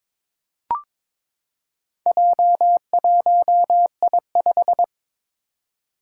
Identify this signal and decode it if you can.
Morse code, heard as a keyed tone, about 22 wpm, 700 Hz. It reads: J1I5
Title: Four Challenging Letter-Number Combinations 22wpm